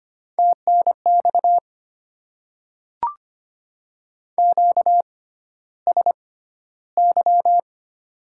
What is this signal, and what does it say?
Morse code, heard as a keyed tone, about 25 wpm, spaced as for 13 wpm, 690 Hz. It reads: TNX QSY
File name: Top 90 QSO Elements - 6x Character Spacing - 25wpm